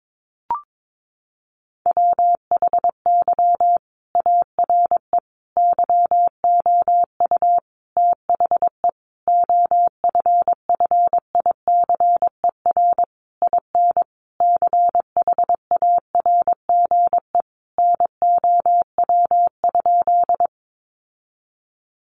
Morse code, heard as a keyed tone, about 22 wpm, 700 Hz. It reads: WHY ARE YOU THE OFFICER IN CHARGE NOW?
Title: Sentences from Top 1200 Words 22wpm